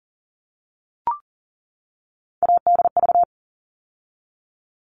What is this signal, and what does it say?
Morse code, heard as a keyed tone, about 40 wpm, 700 Hz. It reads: ADV
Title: All Cut Numbers - 3-Digit 40wpm